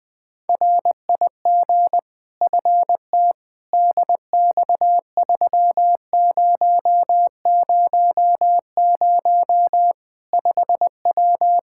Morse code, heard as a keyed tone, about 20 wpm, 700 Hz. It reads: RIG FT DX3000 5W
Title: K1USN SST QSO Practice 20wpm